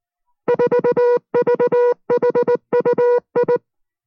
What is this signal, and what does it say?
Morse code, heard as a keyed tone, about 19 wpm, 490 Hz. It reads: 4VHUI